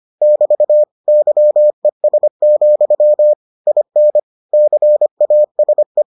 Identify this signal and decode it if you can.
Morse code, heard as a keyed tone, about 25 wpm, 600 Hz. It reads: <BT> YES, IN CASE